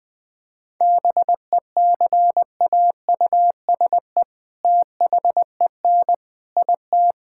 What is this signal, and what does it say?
Morse code, heard as a keyed tone, about 20 wpm, 700 Hz. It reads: BECAUSE THEN IT
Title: All-Age Appropriate Jokes - 20wpm